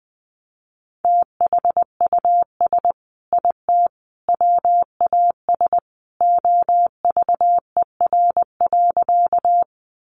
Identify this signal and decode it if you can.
Morse code, heard as a keyed tone, about 20 wpm, 700 Hz. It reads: THUS IT WAS OVER.
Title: Sentences from Top 800 Words 20wpm